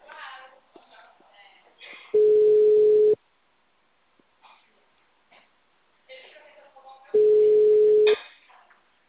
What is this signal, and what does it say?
Ringback tone (Europe, 425 Hz), with speech.